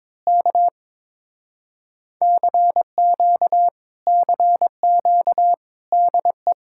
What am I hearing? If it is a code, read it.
Morse code, heard as a keyed tone, about 22 wpm, 700 Hz. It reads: K CQ CQ DE
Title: Speed-Racing Common QSO Phrases - 22wpm